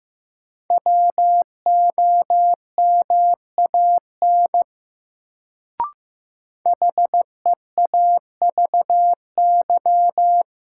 Morse code, heard as a keyed tone, about 15 wpm, 700 Hz. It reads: WOMAN HEAVY
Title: Sentences from Top 1100 Words - New Word Drill Alt - 15wpm